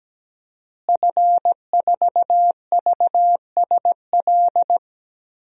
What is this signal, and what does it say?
Morse code, heard as a keyed tone, about 17 wpm, 700 Hz. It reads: F4VSL
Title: DXCC-100 Call Signs - 5 Characters - Repeated 2x - 17wpm